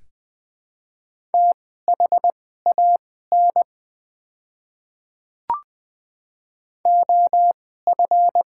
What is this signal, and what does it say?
Morse code, heard as a keyed tone, about 20 wpm, spaced as for 14 wpm, 700 Hz. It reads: THAN OF
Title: 2x Character Spacing - Top 100 Words - 20wpm